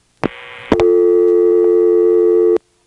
Dial tone (North America), 350 + 440 Hz.